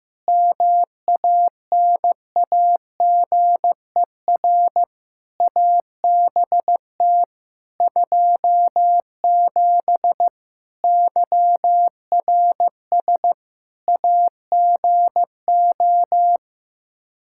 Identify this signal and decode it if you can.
Morse code, heard as a keyed tone, about 15 wpm, 700 Hz. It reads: MANAGER ABT 27 YRS AGO
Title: Practice QSO - Exchange 3 - More Info - 15wpm